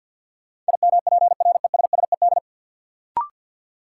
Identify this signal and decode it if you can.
Morse code, heard as a keyed tone, about 52 wpm, 700 Hz. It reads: IMPRESSED
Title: Sentences from Top 1600 Words - New Word Drill - 50wpm